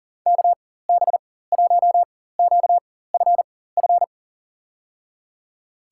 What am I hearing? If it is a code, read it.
Morse code, heard as a keyed tone, about 40 wpm, spaced as for 19 wpm, 700 Hz. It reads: KB1QFF
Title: USA Call Signs - 6 Characters - 4x Character Spacing - 40wpm